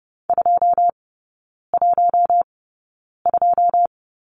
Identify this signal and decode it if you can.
Morse code, heard as a keyed tone, about 30 wpm, spaced as for 9 wpm, 700 Hz. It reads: 212
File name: course / 30:10wpm / 087 Lesson 15e - Number 5 - Three Characters 30wpm